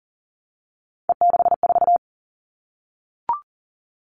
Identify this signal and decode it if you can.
Morse code, heard as a keyed tone, about 40 wpm, 700 Hz. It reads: E64
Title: All Cut Numbers - 3-Digit 40wpm